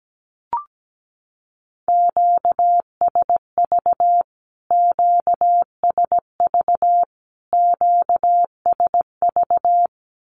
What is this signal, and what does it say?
Morse code, heard as a keyed tone, about 17 wpm, 700 Hz. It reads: QSV QSV QSV
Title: Q-Codes Repeated 3x 17wpm